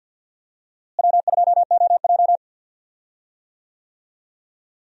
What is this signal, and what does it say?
Morse code, heard as a keyed tone, about 51 wpm, 700 Hz. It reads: W2OJ
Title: USA Call Signs - 4 Characters - Repeated 3x - 50wpm